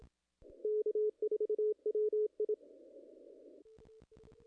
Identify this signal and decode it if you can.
Morse code, heard as a keyed tone, about 26 wpm, 430 Hz. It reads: K4WI
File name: contests / ss-cw-k5tr-op-n5rz -0000